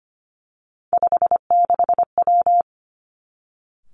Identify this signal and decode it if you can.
Morse code, heard as a keyed tone, about 25 wpm, 700 Hz. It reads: 56W
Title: Three Challenging Letter-Number Combinations 25wpm